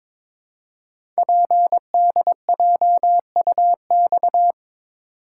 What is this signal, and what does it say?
Morse code, heard as a keyed tone, about 23 wpm, 700 Hz. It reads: PDJUX
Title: Code Groups - 5 Letters - 22wpm